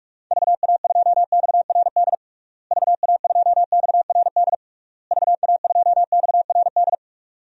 Morse code, heard as a keyed tone, about 46 wpm, 700 Hz. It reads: VA2XRD VA2XRD VA2XRD
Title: DXCC-100 Call Signs - 6 Characters - Repeated 3x - 45wpm